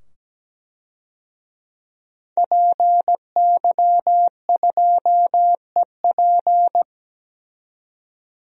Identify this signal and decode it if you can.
Morse code, heard as a keyed tone, about 17 wpm, 700 Hz. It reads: PY2EP
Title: DXCC-100 Call Signs - 5 Characters - 17wpm